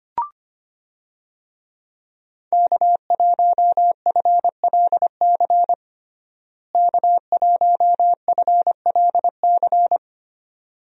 Morse code, heard as a keyed tone, about 25 wpm, 700 Hz. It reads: K1FLC K1FLC
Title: USA Call Signs - 5 Characters - Repeated 2x - 25wpm